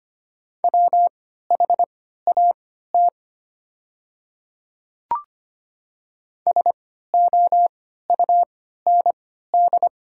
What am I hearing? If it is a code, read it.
Morse code, heard as a keyed tone, about 25 wpm, spaced as for 14 wpm, 690 Hz. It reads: WHAT SOUND
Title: 3x Character Spacing - Top 100 Words - 25wpm